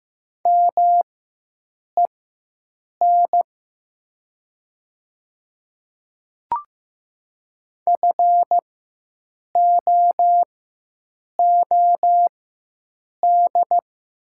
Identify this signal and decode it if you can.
Morse code, heard as a keyed tone, about 15 wpm, spaced as for 7 wpm, 700 Hz. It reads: MEN FOOD
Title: 4x Character Spacing - Top 300 Words - 15wpm